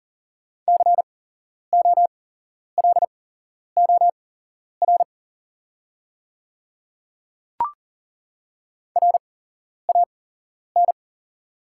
Morse code, heard as a keyed tone, about 39 wpm, spaced as for 11 wpm, 700 Hz. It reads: COLOR RAN
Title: 8x Character Spacing - Top 500 Words - 40wpm